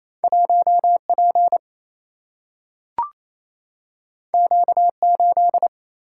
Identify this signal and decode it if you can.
Morse code, heard as a keyed tone, about 28 wpm, 700 Hz. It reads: 1 P Q 8
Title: Single-Letter Single-Number 28wpm